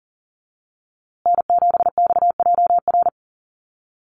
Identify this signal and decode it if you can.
Morse code, heard as a keyed tone, about 40 wpm, 700 Hz. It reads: N7XJR
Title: USA Call Signs Repeated 3x 40wpm